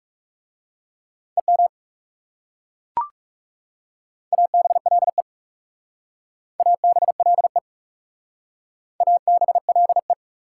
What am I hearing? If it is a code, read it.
Morse code, heard as a keyed tone, about 40 wpm, 700 Hz. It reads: EM ABLE ABLE ABLE